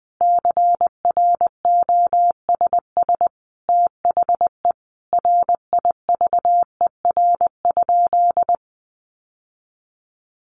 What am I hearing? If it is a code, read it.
Morse code, heard as a keyed tone, about 19 wpm, 700 Hz. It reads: CROSS THE RIVER?
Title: Sentences from Top 1400 Words 20wpm